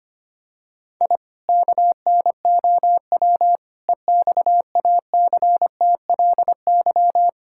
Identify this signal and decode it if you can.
Morse code, heard as a keyed tone, about 25 wpm, 700 Hz. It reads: I KNOW EXACTLY
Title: Sentences from Top 1000 Words - Review Part 6 - 25wpm